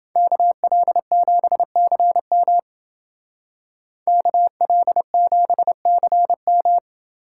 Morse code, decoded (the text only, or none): KL7CM KL7CM